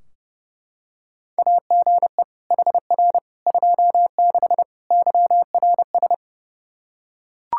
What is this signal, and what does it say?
Morse code, heard as a keyed tone, about 32 wpm, 700 Hz. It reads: AGE HR 26 YRS
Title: Practice QSO - Exchange 3 - More Info - 30wpm